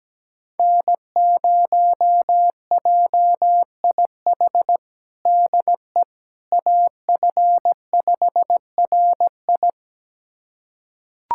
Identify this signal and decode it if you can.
Morse code, heard as a keyed tone, about 17 wpm, 700 Hz. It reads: N0JIH DE AF5RI
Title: USA Callsign DE Callsign - 5 Characters - 17wpm